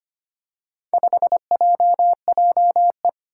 Morse code, heard as a keyed tone, about 25 wpm, 700 Hz. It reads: 5JJE
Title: Four Challenging Letter-Number Combinations 25wpm